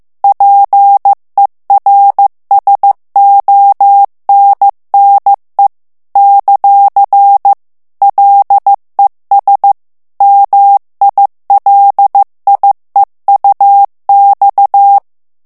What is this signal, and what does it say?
Morse code, heard as a keyed tone, about 15 wpm, 800 Hz. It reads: PERSONNE ; LES MILIEUX
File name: morse mp3 / FRANCAIS TEXTE CLAIR 2 VITESSE 14 MOTS MINUTES